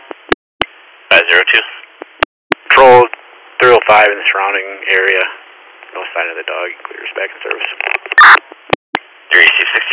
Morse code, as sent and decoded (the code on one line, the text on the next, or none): none